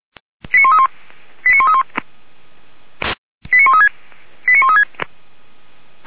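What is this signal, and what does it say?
Five-tone selective call (ZVEI), heard as quick runs of five tones, tones of about 70 ms. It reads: 89131 89131 89136 89136
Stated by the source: ZVEI five-tone call 89200 received by SDR, recorded by feuerwehr-orlamuende (fire brigade Orlamünde).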